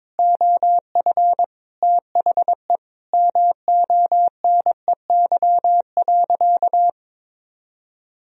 Morse code, heard as a keyed tone, about 22 wpm, 700 Hz. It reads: OF THE MONEY.
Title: Sentences from Top 1000 Words - Review Part 2 - 22wpm